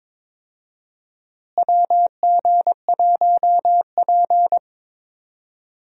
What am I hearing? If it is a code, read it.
Morse code, heard as a keyed tone, about 22 wpm, 700 Hz. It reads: WG1P
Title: USA Call Signs - 4 Characters - Repeated 2x - 22wpm